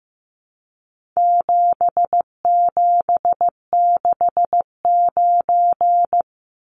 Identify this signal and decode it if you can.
Morse code, heard as a keyed tone, about 15 wpm, 700 Hz. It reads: 7769
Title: Numbers - 4-Digit 15wpm